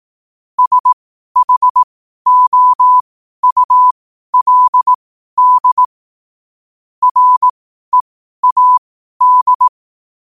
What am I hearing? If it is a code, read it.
Morse code, heard as a keyed tone, about 18 wpm, spaced as for 12 wpm, 1000 Hz. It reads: SHOULD READ
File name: qotdfiles / 12wpm